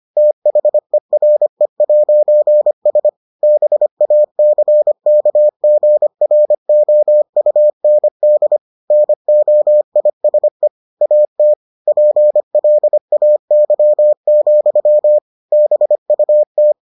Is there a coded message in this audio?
Yes: Morse code from 0.2 to 16.7 s, heard as a keyed tone, about 25 wpm, 600 Hz. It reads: THERE'S BACKGROUND NOISE AT PLAY, BUT